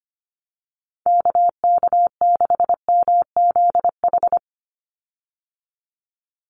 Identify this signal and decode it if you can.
Morse code, heard as a keyed tone, about 25 wpm, 700 Hz. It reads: KK6MZH
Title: USA Call Signs 25wpm